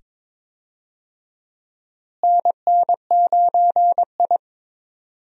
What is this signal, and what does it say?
Morse code, heard as a keyed tone, about 21 wpm, 700 Hz. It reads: NN9I
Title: USA Call Signs - 4 Characters - Repeated 2x - 22wpm